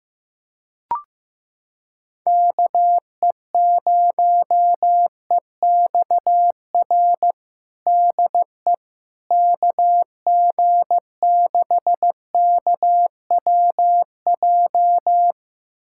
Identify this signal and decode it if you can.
Morse code, heard as a keyed tone, about 16 wpm, 700 Hz. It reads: KE0EXR DE KG6KWJ
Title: USA Callsign DE Callsign - 6 Characters - 15wpm